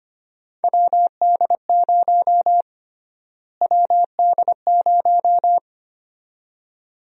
Morse code, heard as a keyed tone, about 25 wpm, 700 Hz. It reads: WD0 WD0
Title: USA Call Sign Prefixes - Repeated 2x - 25wpm